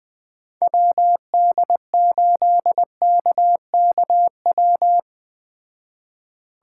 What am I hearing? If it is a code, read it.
Morse code, heard as a keyed tone, about 19 wpm, 700 Hz. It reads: WD8KKW
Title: Speed Racing - USA 2x3 Call Signs 17wpm